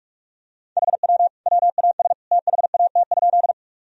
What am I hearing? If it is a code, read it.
Morse code, heard as a keyed tone, about 43 wpm, 700 Hz. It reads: HW WAS THAT?
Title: Sentences from Top 100 Words and Abbreviations 45wpm